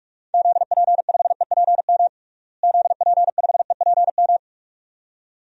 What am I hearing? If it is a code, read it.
Morse code, heard as a keyed tone, about 48 wpm, 700 Hz. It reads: ZP5EPM ZP5EPM
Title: DXCC-100 Call Signs - 6 Characters - Repeated 2x - 45wpm